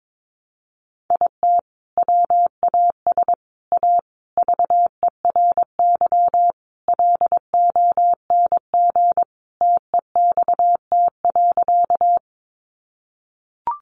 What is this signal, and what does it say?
Morse code, heard as a keyed tone, about 22 wpm, 700 Hz. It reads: IT WAS A VERY LONG TEXT.
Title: Sentences from Top 1200 Words 22wpm